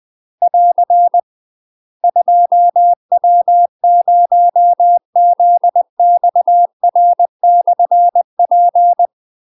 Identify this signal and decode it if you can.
Morse code, heard as a keyed tone, about 20 wpm, 700 Hz. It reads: <AR> 2W0ZXR/P
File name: UK MASTER QSOs TRAINING / 20WPM / UK MASTER QSOs 0019